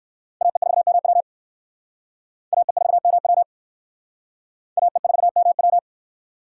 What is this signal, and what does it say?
Morse code, heard as a keyed tone, about 55 wpm, 700 Hz. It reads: AE4MW AE4MW AE4MW
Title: Speed Racing - USA Call Signs 45wpm